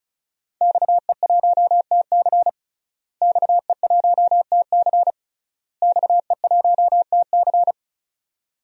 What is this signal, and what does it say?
Morse code, heard as a keyed tone, about 36 wpm, 700 Hz. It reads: XE1TC XE1TC XE1TC
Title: DXCC-100 Call Signs - 5 Characters - Repeated 3x - 35wpm